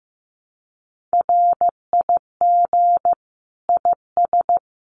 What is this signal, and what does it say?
Morse code, heard as a keyed tone, about 15 wpm, 700 Hz. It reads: RIG IS
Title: Common QSO Phrases Encore 15wpm